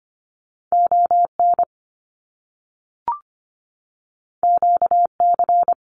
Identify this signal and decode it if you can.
Morse code, heard as a keyed tone, about 25 wpm, 700 Hz. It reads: ON QC
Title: Canadian Provinces and Territories 25wpm